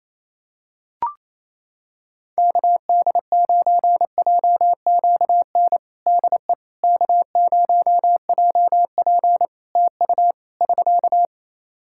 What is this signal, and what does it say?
Morse code, heard as a keyed tone, about 27 wpm, 700 Hz. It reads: KD9JQN DE K0JP TU <SK>
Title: Practice QSO - Exchange 4 - Ending - 28wpm